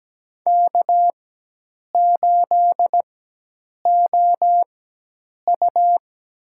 Morse code, heard as a keyed tone, about 17 wpm, spaced as for 8 wpm, 700 Hz. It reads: K8OU
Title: USA Call Signs - 4 Characters - 4x Character Spacing - 17wpm